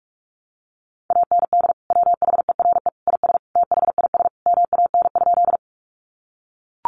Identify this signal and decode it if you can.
Morse code, heard as a keyed tone, about 44 wpm, 700 Hz. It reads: AND WHERE IS THIS MAN?
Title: Sentences from Top 200 Words 45wpm